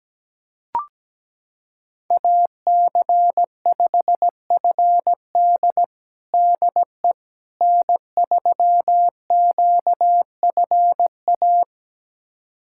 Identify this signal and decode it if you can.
Morse code, heard as a keyed tone, about 17 wpm, 700 Hz. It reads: AC5FD DE N3QFA
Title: USA Callsign DE Callsign - 5 Characters - 17wpm